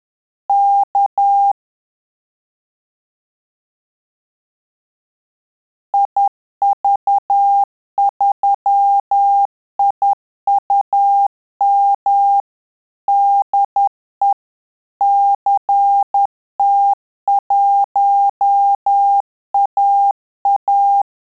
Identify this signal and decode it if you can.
Morse code, heard as a keyed tone, about 10 wpm, 800 Hz. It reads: K IV3IUM DE CT1AA
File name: Telegraphy-group / QSO CW 50 CPM